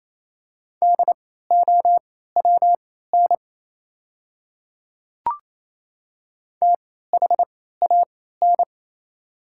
Morse code, heard as a keyed tone, about 28 wpm, spaced as for 16 wpm, 700 Hz. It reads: DOWN THAN
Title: 3x Character Spacing - Top 100 Words - 28wpm